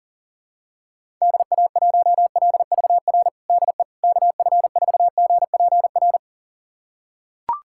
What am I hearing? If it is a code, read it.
Morse code, heard as a keyed tone, about 40 wpm, 700 Hz. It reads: DA1LVR DE KF4GPR
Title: USA Callsign DE Callsign - 6 Characters - 40wpm